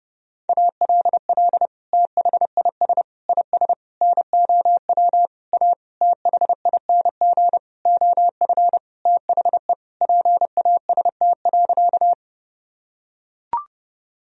Morse code, heard as a keyed tone, about 30 wpm, 700 Hz. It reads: ALL THIS IS NOW A THING OF THE PAST.
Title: Sentences from Top 1000 Words - Review Part 3 - 30wpm